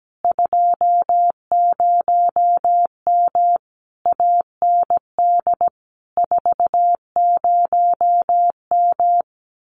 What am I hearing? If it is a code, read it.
Morse code, heard as a keyed tone, about 17 wpm, 700 Hz. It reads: 20M AND 40M
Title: Common QSO Phrases Encore 17wpm